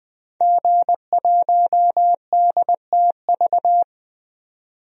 Morse code, heard as a keyed tone, about 20 wpm, 700 Hz. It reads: G1DTV
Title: DXCC-100 Call Signs - 5 Characters - 20wpm